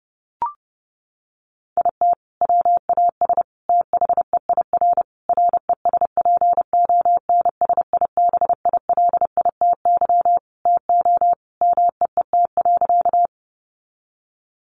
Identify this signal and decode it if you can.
Morse code, heard as a keyed tone, about 28 wpm, 700 Hz. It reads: IT WAS THEIR RESPONSIBILITY TO MEET.
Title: Sentences from Top 1200 Words 30wpm